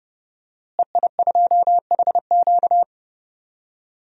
Morse code, heard as a keyed tone, about 30 wpm, 700 Hz. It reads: EI2HQ
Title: DXCC-100 Call Signs - 5 Characters - Repeated 3x - 30wpm